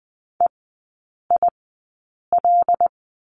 Morse code, heard as a keyed tone, about 20 wpm, spaced as for 8 wpm, 700 Hz. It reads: EIL
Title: 107 Lesson 18e - Letter m - Three Characters 20wpm